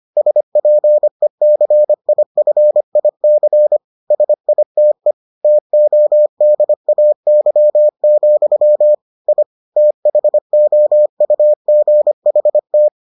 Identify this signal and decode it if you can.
Morse code, heard as a keyed tone, about 25 wpm, 600 Hz. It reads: SPECIFIC SITE TODAY, I THOUGHT